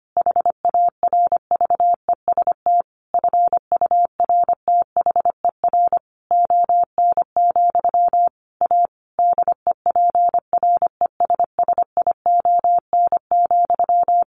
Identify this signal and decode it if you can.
Morse code, heard as a keyed tone, about 25 wpm, 700 Hz. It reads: HARVEST FURTHER ON, A DEPRESSION,
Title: A Sound Of Thunder 25wpm